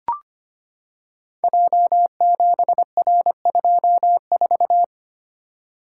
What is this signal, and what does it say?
Morse code, heard as a keyed tone, about 25 wpm, 700 Hz. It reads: J7R24
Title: Code Groups - 5 Letter:Numbers - 25wpm